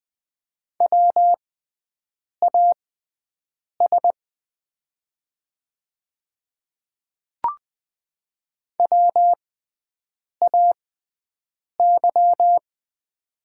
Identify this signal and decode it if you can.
Morse code, heard as a keyed tone, about 20 wpm, spaced as for 7 wpm, 690 Hz. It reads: WAS WAY
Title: 6x Character Spacing - Top 300 Words - 20wpm